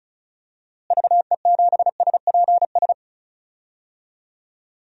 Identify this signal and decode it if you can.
Morse code, heard as a keyed tone, about 36 wpm, 700 Hz. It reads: VE7SPS
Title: DXCC-100 Call Signs - 6 Characters - Repeated 2x - 35wpm